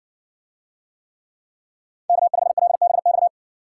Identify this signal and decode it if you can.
Morse code, heard as a keyed tone, about 55 wpm, 700 Hz. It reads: K5LBX